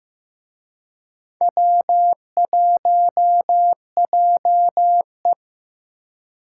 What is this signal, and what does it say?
Morse code, heard as a keyed tone, about 18 wpm, 700 Hz. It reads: W1JE
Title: Speed Racing - USA 1x2 Call Signs 15wpm